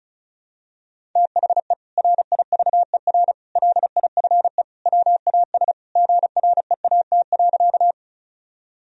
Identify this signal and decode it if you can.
Morse code, heard as a keyed tone, about 36 wpm, 700 Hz. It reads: THE RIVER LIFE WAS GREAT.
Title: Sentences from Top 1000 Words - Review Part 1 - 35wpm